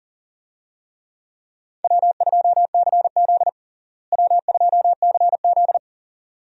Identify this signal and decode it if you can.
Morse code, heard as a keyed tone, about 40 wpm, 700 Hz. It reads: W2CZ W2CZ
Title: USA Call Signs - 4 Characters - Repeated 2x - 40wpm